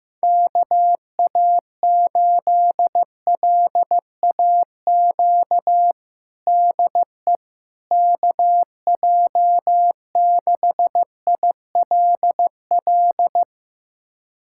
Morse code, heard as a keyed tone, about 15 wpm, 700 Hz. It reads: KA8LAQ DE KJ6ILL
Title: USA Callsign DE Callsign 15wpm